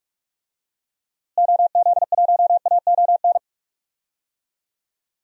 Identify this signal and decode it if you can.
Morse code, heard as a keyed tone, about 46 wpm, 700 Hz. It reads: OZ1AON
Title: DXCC-100 Call Signs - 6 Characters - Repeated 3x - 45wpm